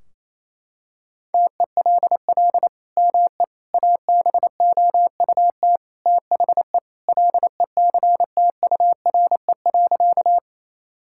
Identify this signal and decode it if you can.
Morse code, heard as a keyed tone, about 28 wpm, 700 Hz. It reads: TELL ME ABOUT THE LECTURE.
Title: Sentences from Top 1600 Words 28wpm